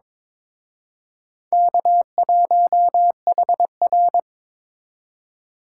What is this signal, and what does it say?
Morse code, heard as a keyed tone, about 26 wpm, 700 Hz. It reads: K1HR
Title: Speed Racing - USA 1x2 Call Signs 22wpm